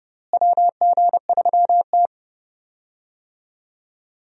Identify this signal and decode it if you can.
Morse code, heard as a keyed tone, about 30 wpm, 700 Hz. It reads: WG3T
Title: USA Call Signs - 4 Characters - Repeated 2x - 30wpm